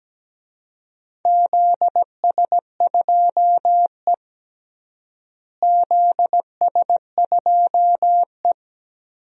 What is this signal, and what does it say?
Morse code, heard as a keyed tone, about 17 wpm, 700 Hz. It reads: ZS2E ZS2E
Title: DXCC-100 Call Signs - 4 Characters - Repeated 2x - 17wpm